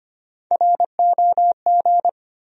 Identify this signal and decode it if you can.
Morse code, heard as a keyed tone, about 25 wpm, 700 Hz. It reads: ROG
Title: USA Common Names - 25wpm